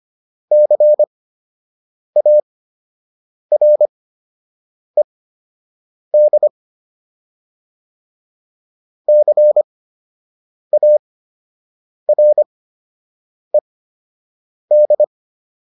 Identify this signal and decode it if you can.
Morse code, heard as a keyed tone, about 25 wpm, spaced as for 7 wpm, 600 Hz. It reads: CARED CARED